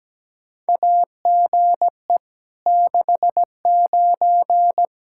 Morse code, heard as a keyed tone, about 17 wpm, 700 Hz. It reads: AGE 69
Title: Practice QSO - Exchange 3 - More Info - 17wpm